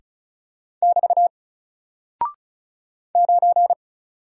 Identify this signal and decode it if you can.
Morse code, heard as a keyed tone, about 36 wpm, 700 Hz. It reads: <BT> 9
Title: Single Character (Letter:Number:Punctuation:Prosign) Alt 35wpm